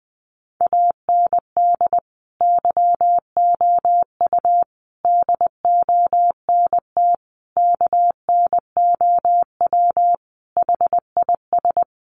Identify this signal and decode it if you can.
Morse code, heard as a keyed tone, about 20 wpm, 700 Hz. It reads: AND YOU DONT KNOW HIS
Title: Sentences from Top 1000 Words 20wpm